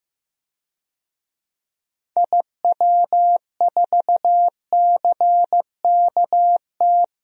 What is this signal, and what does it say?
Morse code, heard as a keyed tone, about 15 wpm, 700 Hz. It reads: IW4CKT
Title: DXCC-100 Call Signs - 6 Characters - 15wpm